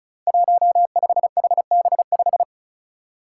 Morse code, heard as a keyed tone, about 34 wpm, 700 Hz. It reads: 15HB5
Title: Five Challenging Letter-Number Combinations 35wpm